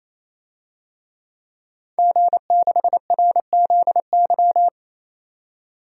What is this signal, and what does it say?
Morse code, heard as a keyed tone, about 28 wpm, 700 Hz. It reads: G6RZY
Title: DXCC-100 Call Signs - 5 Characters - 28wpm